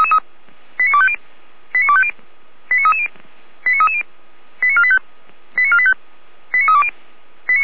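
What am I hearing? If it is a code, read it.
Five-tone selective call (ZVEI), heard as quick runs of five tones, tones of about 70 ms. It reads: 89250 89250 89330 89330 89485 89485 89320
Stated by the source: ZVEI five-tone selective call received by SDR, recorded by Feuerwehr Orlamünde.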